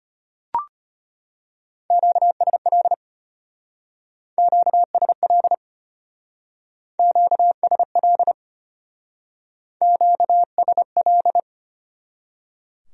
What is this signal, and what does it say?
Morse code, heard as a keyed tone, about 30 wpm, 700 Hz. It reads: QSL QSL QSL QSL
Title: Speed-Racing Q-Codes - 25wpm